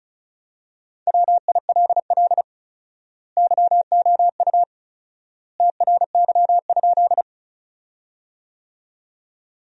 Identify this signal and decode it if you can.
Morse code, heard as a keyed tone, about 36 wpm, 700 Hz. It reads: WILL YOU TRY?
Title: 4x Word-Spacing - Sentences from Top 200 Words - 35wpm